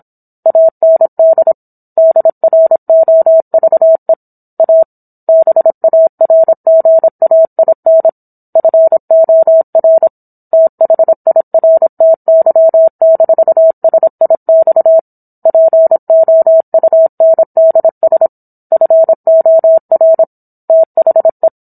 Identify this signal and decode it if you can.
Morse code, heard as a keyed tone, about 27 wpm, 650 Hz. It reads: AND DROVE A BARGAIN FOR THIRTY-SIX POUNDS FOR THE